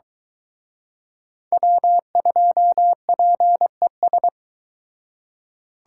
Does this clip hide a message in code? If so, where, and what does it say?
Yes: Morse code from 1.5 to 4.3 s, heard as a keyed tone, about 23 wpm, 700 Hz. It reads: W2PES